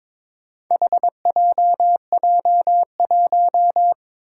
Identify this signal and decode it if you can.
Morse code, heard as a keyed tone, about 22 wpm, 700 Hz. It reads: HJJ1
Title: Four Challenging Letter-Number Combinations 22wpm